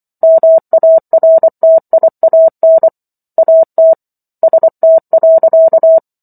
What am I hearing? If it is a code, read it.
Morse code, heard as a keyed tone, about 24 wpm, 650 Hz. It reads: MARTIAN AT ST.